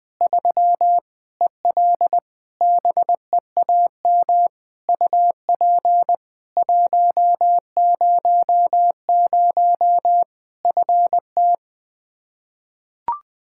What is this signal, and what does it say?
Morse code, heard as a keyed tone, about 20 wpm, 700 Hz. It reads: 3 EL BEAM UP 100 FT
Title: K1USN SST QSO Practice 20wpm